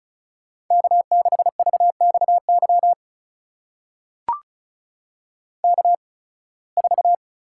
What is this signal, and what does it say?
Morse code, heard as a keyed tone, about 34 wpm, spaced as for 23 wpm, 700 Hz. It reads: K6VXY K4
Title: USA Call Signs - 8x Character Spacing - 35wpm